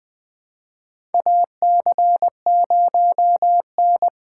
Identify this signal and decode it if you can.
Morse code, heard as a keyed tone, about 20 wpm, 700 Hz. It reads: AC0N
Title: USA Callsign DE Callsign 20wpm